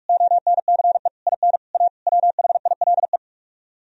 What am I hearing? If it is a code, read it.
Morse code, heard as a keyed tone, about 44 wpm, 700 Hz. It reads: ONCE IN A WHILE